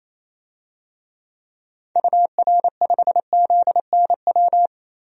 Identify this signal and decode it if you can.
Morse code, heard as a keyed tone, about 28 wpm, 700 Hz. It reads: UR5ZNW
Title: DXCC-100 Call Signs - 6 Characters - Repeated 3x - 28wpm